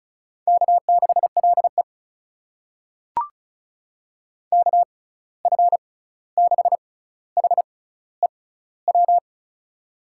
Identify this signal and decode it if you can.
Morse code, heard as a keyed tone, about 36 wpm, spaced as for 16 wpm, 700 Hz. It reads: K6LE KF6HEW
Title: USA Call Signs - 6x Character Spacing - 35wpm